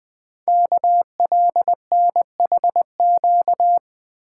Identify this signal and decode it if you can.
Morse code, heard as a keyed tone, about 20 wpm, 700 Hz. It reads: KLNHQ
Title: Code Groups - 5 Letters - 20wpm